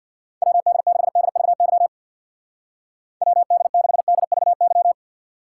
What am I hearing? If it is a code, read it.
Morse code, heard as a keyed tone, about 63 wpm, 700 Hz. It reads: WD6DVY WD6DVY